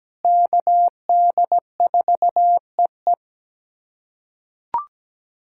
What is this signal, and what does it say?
Morse code, heard as a keyed tone, about 17 wpm, 700 Hz. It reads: KD4EE
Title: K1USN SST QSO Practice - 17wpm